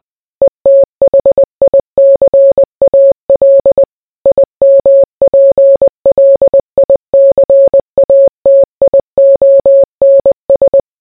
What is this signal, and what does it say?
Morse code, heard as a keyed tone, about 20 wpm, 550 Hz. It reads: ETHICAL IMPLICATIONS